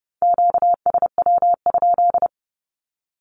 Morse code, heard as a keyed tone, about 30 wpm, 700 Hz. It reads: QSW?
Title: Q-Codes 30wpm